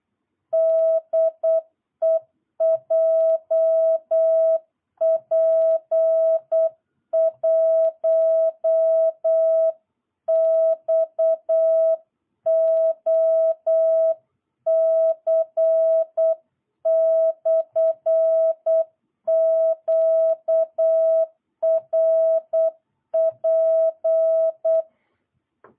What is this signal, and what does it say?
Morse code, heard as a keyed tone, about 8 wpm, 660 Hz. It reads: DE JP1XOC/QRP
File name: jh8cbh / 2022cw